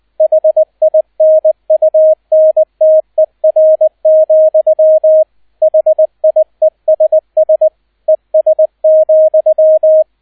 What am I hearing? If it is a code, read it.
Morse code, heard as a keyed tone, about 20 wpm, 630 Hz, with radio noise. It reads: HINUNTER, HIESS ES,